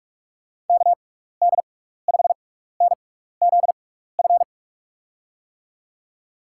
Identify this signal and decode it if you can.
Morse code, heard as a keyed tone, about 44 wpm, spaced as for 15 wpm, 700 Hz. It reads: KD5NZF
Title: USA Call Signs - 6 Characters - 6x Character Spacing - 45wpm